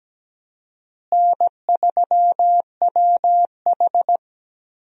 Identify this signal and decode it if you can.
Morse code, heard as a keyed tone, about 17 wpm, 700 Hz. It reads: N3WH
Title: USA Call Signs - 4 Characters - Repeated 3x - 17wpm